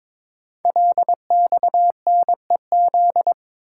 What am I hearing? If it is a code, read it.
Morse code, heard as a keyed tone, about 23 wpm, 700 Hz. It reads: LXNEZ
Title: Code Groups - 5 Letters - 22wpm